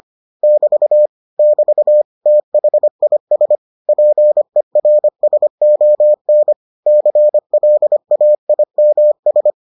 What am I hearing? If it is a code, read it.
Morse code, heard as a keyed tone, about 25 wpm, 600 Hz. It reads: <BT> <BT> THIS PERSON CLAIMS